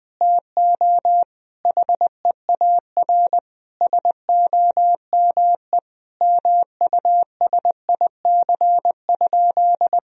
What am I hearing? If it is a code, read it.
Morse code, heard as a keyed tone, about 21 wpm, 700 Hz. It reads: TO HEAR SOME MUSIC?
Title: Sentences from Top 1000 Words - Review Part 1 - 20wpm